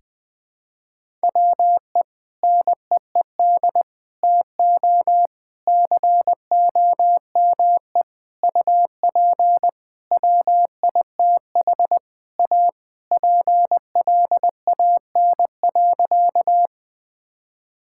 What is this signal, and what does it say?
Morse code, heard as a keyed tone, about 20 wpm, 700 Hz. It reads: WE NEED TO COME UP WITH A PLAN.
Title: Sentences with 5-Word Phrases - Spoken Beginning - 20wpm